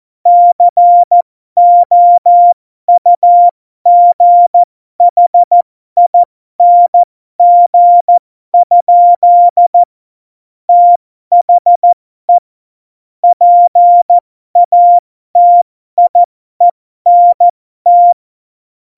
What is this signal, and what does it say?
Morse code, heard as a keyed tone, about 14 wpm, spaced as for 12 wpm, 700 Hz. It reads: COUGHING? THE PATIENT
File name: Morsekurs / Witze